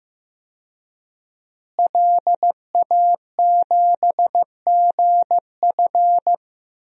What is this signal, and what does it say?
Morse code, heard as a keyed tone, about 15 wpm, 700 Hz. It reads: LA7GF
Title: DXCC-100 Call Signs - 5 Characters - Repeated 3x - 15wpm